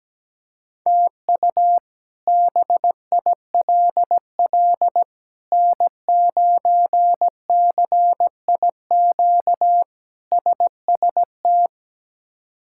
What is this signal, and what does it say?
Morse code, heard as a keyed tone, about 17 wpm, 700 Hz. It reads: TU BILL N9CIQ SST
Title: K1USN SST QSO Practice - 17wpm